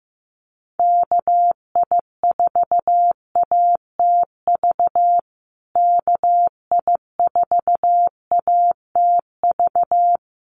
Morse code, heard as a keyed tone, about 15 wpm, 700 Hz. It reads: KI4ATV KI4ATV
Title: USA Call Signs Repeated 3x 15wpm